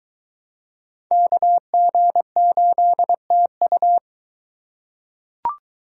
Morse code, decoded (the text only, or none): KG8TU